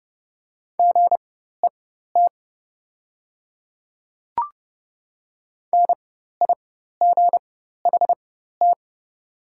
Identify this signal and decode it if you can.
Morse code, heard as a keyed tone, about 30 wpm, spaced as for 14 wpm, 700 Hz. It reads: GET NIGHT
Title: 4x Character Spacing - Top 300 Words - 30wpm